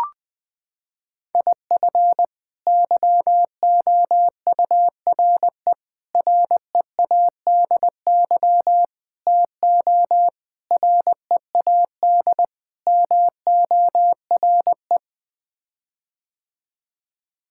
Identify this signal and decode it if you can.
Morse code, heard as a keyed tone, about 21 wpm, 700 Hz. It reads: IF YOURE READY TO READ MORE
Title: Sets of 6 Words - 20wpm